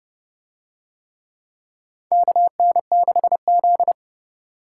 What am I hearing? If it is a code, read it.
Morse code, heard as a keyed tone, about 29 wpm, 700 Hz. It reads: KN6Z